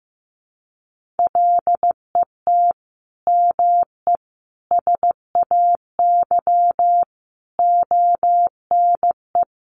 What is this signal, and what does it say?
Morse code, heard as a keyed tone, about 15 wpm, 700 Hz. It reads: LET ME SAY ONE
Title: Sentences from Top 300 Words 15wpm